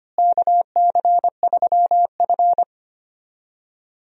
Morse code, heard as a keyed tone, about 25 wpm, 700 Hz. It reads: KC3F
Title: USA Call Signs - 4 Characters - Repeated 3x - 25wpm